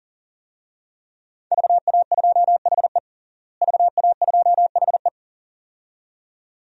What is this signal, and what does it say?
Morse code, heard as a keyed tone, about 40 wpm, 700 Hz. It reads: VA2HE VA2HE
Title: DXCC-100 Call Signs - 5 Characters - Repeated 2x - 40wpm